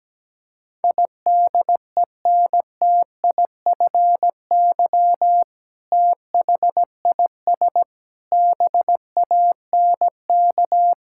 Morse code, heard as a keyed tone, about 17 wpm, 700 Hz. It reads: IDENTIFY THIS BANK
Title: Sentences from Top 1000 Words - Review Part 6 - 17wpm